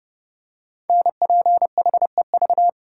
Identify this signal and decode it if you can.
Morse code, heard as a keyed tone, about 30 wpm, 700 Hz. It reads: NPHEV
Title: Code Groups - 5 Letters - 30wpm